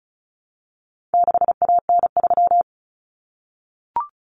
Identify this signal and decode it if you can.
Morse code, heard as a keyed tone, about 34 wpm, 700 Hz. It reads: 6AN3
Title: Common Cut Numbers - 4-Digit 35wpm